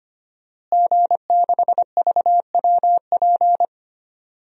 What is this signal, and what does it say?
Morse code, heard as a keyed tone, about 25 wpm, 700 Hz. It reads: G6VWP
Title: DXCC-100 Call Signs - 5 Characters - Repeated 2x - 25wpm